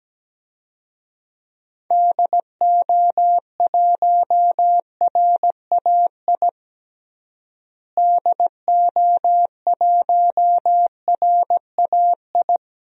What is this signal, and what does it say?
Morse code, heard as a keyed tone, about 17 wpm, 700 Hz. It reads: DO1RAI DO1RAI
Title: DXCC-100 Call Signs - 6 Characters - Repeated 2x - 17wpm